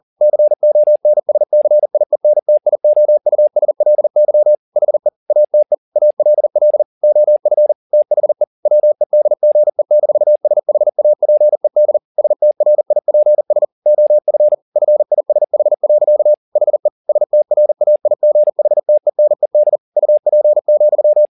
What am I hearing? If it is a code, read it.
Morse code, heard as a keyed tone, about 40 wpm, 600 Hz. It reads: CONSCIENTIOUSLY HE ATE ALL OF THE WEDGE-SHAPED STRIPS OF FISH. HE STRAIGHTENED UP,